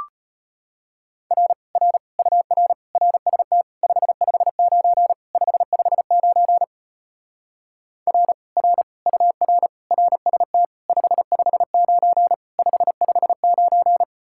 Morse code, heard as a keyed tone, about 33 wpm, 700 Hz. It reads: R R UR RST 559 559 R R UR RST 559 559